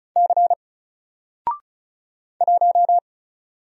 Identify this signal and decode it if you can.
Morse code, heard as a keyed tone, about 36 wpm, 700 Hz. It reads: C 1
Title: Single Letter-Number 35wpm